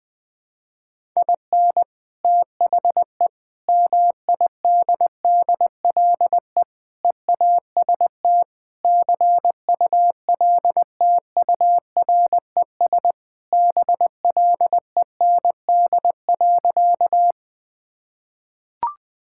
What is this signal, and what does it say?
Morse code, heard as a keyed tone, about 20 wpm, 700 Hz. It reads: IN THE MIDDLE EAST CULTURES BLEND.
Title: Sentences with 4-Word Phrases - 20wpm